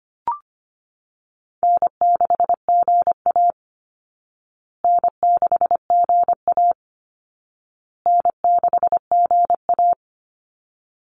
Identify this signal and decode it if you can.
Morse code, heard as a keyed tone, about 25 wpm, 700 Hz. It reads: N6GA N6GA N6GA
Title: Easy Call Signs Repeated 3x - 25wpm